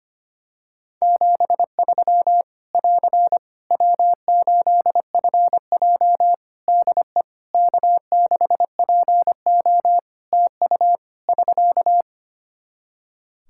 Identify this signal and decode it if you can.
Morse code, heard as a keyed tone, about 25 wpm, 700 Hz. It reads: 73 <AR> W8FJ DE K6PO TU <SK>
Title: Practice QSO - Exchange 4 - Ending - 25wpm